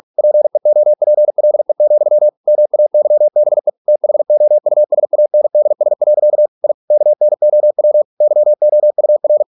Morse code, heard as a keyed tone, about 46 wpm, 600 Hz. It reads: PEOPLE, MAYBE THOUSANDS. I KNOW YOUR